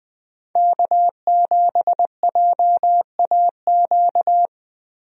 Morse code, heard as a keyed tone, about 20 wpm, 700 Hz. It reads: K7JAQ
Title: USA Call Signs - 5 Characters - Repeated 2x - 20wpm